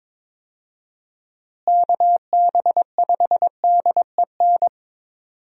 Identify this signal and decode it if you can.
Morse code, heard as a keyed tone, about 21 wpm, 700 Hz. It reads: KB5DEN
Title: USA Call Signs - 6 Characters - Repeated 2x - 22wpm